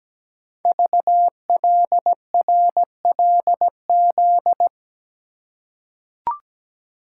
Morse code, heard as a keyed tone, about 17 wpm, 700 Hz. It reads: VLRLZ
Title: Code Groups - 5 Letters - 17wpm